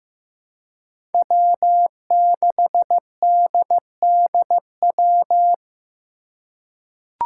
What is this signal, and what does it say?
Morse code, heard as a keyed tone, about 15 wpm, 700 Hz. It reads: W6DDW
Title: Five Challenging Letter-Number Combinations 15wpm